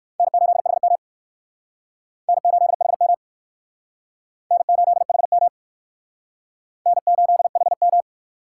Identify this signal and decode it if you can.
Morse code, heard as a keyed tone, about 55 wpm, 700 Hz. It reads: N8HM N8HM N8HM N8HM
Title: Speed Racing - USA 1x2 Call Signs 45wpm